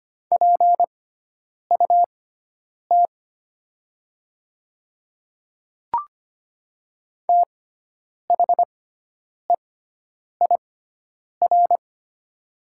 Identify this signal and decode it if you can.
Morse code, heard as a keyed tone, about 25 wpm, spaced as for 9 wpm, 700 Hz. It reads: PUT THEIR
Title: 6x Character Spacing - Top 300 Words - 25wpm